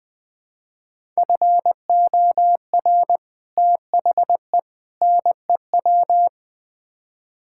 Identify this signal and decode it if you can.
Morse code, heard as a keyed tone, about 20 wpm, 700 Hz. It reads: FOR THE NEW
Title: Build A Phrase - 5-Word Phrases - 20wpm